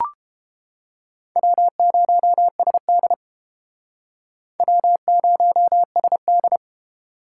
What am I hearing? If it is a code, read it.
Morse code, heard as a keyed tone, about 29 wpm, 700 Hz. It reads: W0SD W0SD